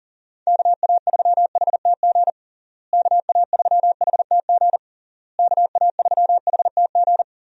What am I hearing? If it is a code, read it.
Morse code, heard as a keyed tone, about 40 wpm, 700 Hz. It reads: KA3HTG KA3HTG KA3HTG
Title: USA Call Signs - 6 Characters - Repeated 3x - 40wpm